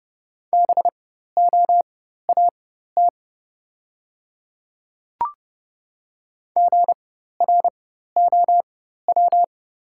Morse code, heard as a keyed tone, about 29 wpm, spaced as for 14 wpm, 700 Hz. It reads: BOAT GROW
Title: 4x Character Spacing - Top 500 Words - 30wpm